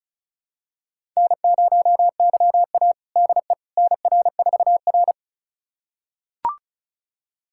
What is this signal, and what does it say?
Morse code, heard as a keyed tone, about 36 wpm, 700 Hz. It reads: N0YA DE NR4R
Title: USA Callsign DE Callsign - 4 Characters - 35wpm